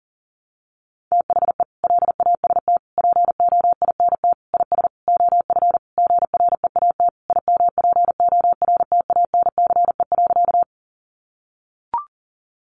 Morse code, heard as a keyed tone, about 40 wpm, 700 Hz. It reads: THE LAST POINT IS OF GREAT IMPORTANCE.
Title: Sentences from Top 1200 Words 40wpm